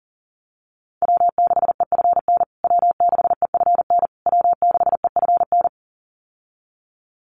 Morse code, heard as a keyed tone, about 40 wpm, 700 Hz. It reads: W6EFN W6EFN W6EFN
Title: USA Call Signs Repeated 3x 40wpm